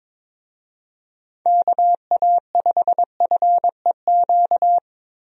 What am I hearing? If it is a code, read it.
Morse code, heard as a keyed tone, about 22 wpm, 700 Hz. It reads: KA5FEQ
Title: USA Call Signs - 6 Characters - Repeated 3x - 22wpm